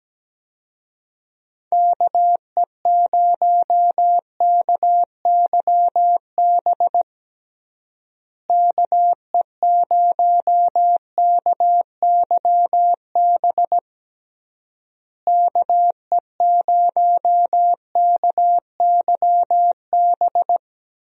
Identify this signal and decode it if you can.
Morse code, heard as a keyed tone, about 17 wpm, 700 Hz. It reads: KE0KYB KE0KYB KE0KYB
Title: USA Call Signs - 6 Characters - Repeated 3x - 17wpm